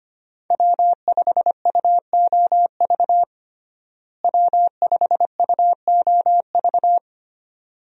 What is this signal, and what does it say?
Morse code, heard as a keyed tone, about 25 wpm, 690 Hz. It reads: W5UOV W5UOV
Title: USA Call Signs - 5 Characters - Repeated 2x - 25wpm